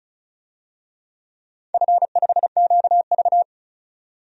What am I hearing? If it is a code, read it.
Morse code, heard as a keyed tone, about 36 wpm, 700 Hz. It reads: F5QV
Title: DXCC-100 Call Signs - 4 Characters - 35wpm